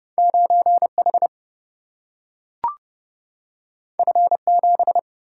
Morse code, heard as a keyed tone, about 31 wpm, 700 Hz. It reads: 9H F7
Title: Single-Letter Single-Number 30wpm